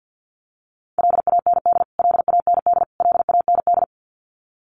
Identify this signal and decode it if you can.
Morse code, heard as a keyed tone, about 50 wpm, 700 Hz. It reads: LAND LAND LAND
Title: Top 500 Words Repeated 50wpm